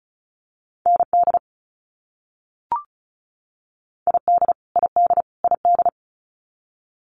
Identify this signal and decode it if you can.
Morse code, heard as a keyed tone, about 34 wpm, 700 Hz. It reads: ND ID ID ID
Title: US States Repeated 3x 35wpm